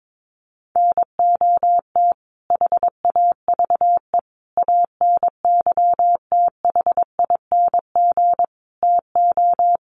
Morse code, heard as a keyed tone, about 22 wpm, 700 Hz. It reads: NOT HAVE ANYTHING TO
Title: Sets of 5 Words Encore 22wpm